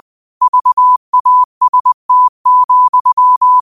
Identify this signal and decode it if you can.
Morse code, heard as a keyed tone, about 20 wpm, 1000 Hz. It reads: VAST,
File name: qotdfiles / 20wpm